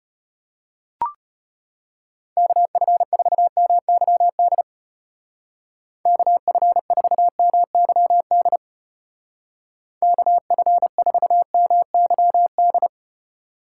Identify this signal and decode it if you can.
Morse code, heard as a keyed tone, about 30 wpm, 700 Hz. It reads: KF4MYD KF4MYD KF4MYD
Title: Speed Racing - USA 2x3 Call Signs 25wpm